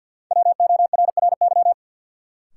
Morse code, heard as a keyed tone, about 52 wpm, 700 Hz. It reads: WORRY
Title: Sentences from Top 1100 Words - New Word Drill - 50wpm